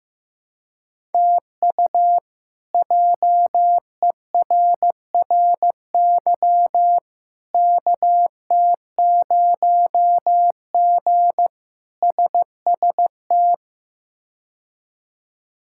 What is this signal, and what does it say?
Morse code, heard as a keyed tone, about 15 wpm, 700 Hz. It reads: TU JERRY KT0G SST
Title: K1USN SST QSO Practice - 15wpm